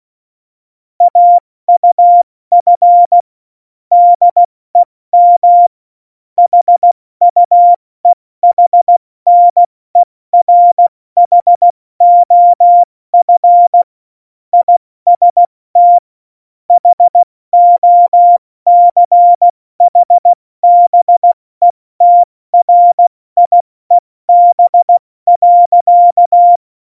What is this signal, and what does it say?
Morse code, heard as a keyed tone, about 16 wpm, 700 Hz. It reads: AUF DEM HUEHNERHOF IST HOCHBETRIEB.
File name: Morsekurs / Witze / Witz 12 Mai, 2020 70 F80